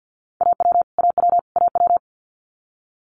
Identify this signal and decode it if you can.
Morse code, heard as a keyed tone, about 50 wpm, 700 Hz. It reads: AW AW AW
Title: Multi-letter Phonemes Repeated 3x 50wpm